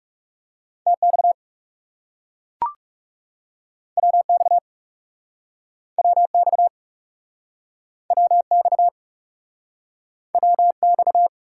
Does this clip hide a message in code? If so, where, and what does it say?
Yes: Morse code from 0.9 to 11.3 s, heard as a keyed tone, about 40 wpm, 700 Hz. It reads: TX WX WX WX WX